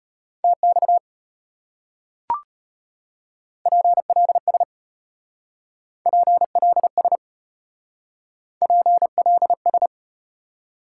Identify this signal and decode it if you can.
Morse code, heard as a keyed tone, about 34 wpm, 700 Hz. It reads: TX PLS PLS PLS